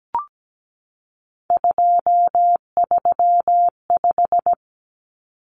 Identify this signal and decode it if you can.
Morse code, heard as a keyed tone, about 17 wpm, 700 Hz. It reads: 235
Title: Common Cut Numbers - 3-Digit 17wpm